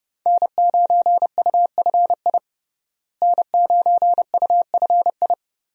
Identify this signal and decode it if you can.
Morse code, heard as a keyed tone, about 30 wpm, 700 Hz. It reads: N9UFI N9UFI
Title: USA Call Signs - 5 Characters - Repeated 3x - 30wpm